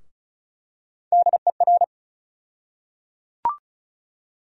Morse code, heard as a keyed tone, about 36 wpm, 700 Hz. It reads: DER
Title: USA Call Sign Postfixes - 3 Letters 35wpm